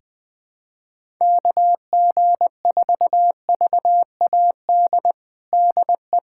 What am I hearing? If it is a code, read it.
Morse code, heard as a keyed tone, about 20 wpm, 700 Hz. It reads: KG4VAD DE
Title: USA Callsign DE Callsign 20wpm